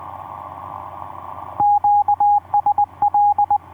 Morse code, heard as a keyed tone, about 20 wpm, 820 Hz, with radio noise. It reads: QSL
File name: v5-dk1ce sq7nsp etc 12m cw 02jan2011 1320